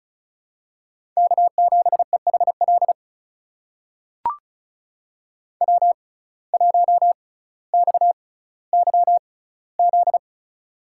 Morse code, heard as a keyed tone, about 36 wpm, spaced as for 18 wpm, 700 Hz. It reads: K7EHL W1XYZ
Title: USA Call Signs - 5 Characters - 6x Character Spacing - 35wpm